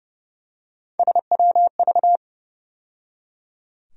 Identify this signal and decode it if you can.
Morse code, heard as a keyed tone, about 30 wpm, 700 Hz. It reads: SWV
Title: USA Call Sign Postfixes - 3 Letters 30wpm